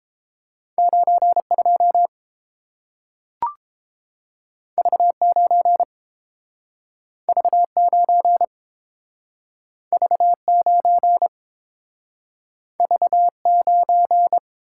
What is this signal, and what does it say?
Morse code, heard as a keyed tone, about 30 wpm, 700 Hz. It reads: 9 2 V 9 V 9 V 9 V 9